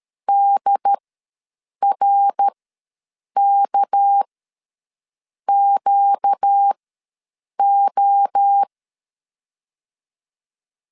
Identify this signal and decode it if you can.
Morse code, heard as a keyed tone, about 13 wpm, spaced as for 7 wpm, 800 Hz. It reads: DRKQO
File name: LEZIONI / 6 PKX0 / 6 PK ws 12wpm 3116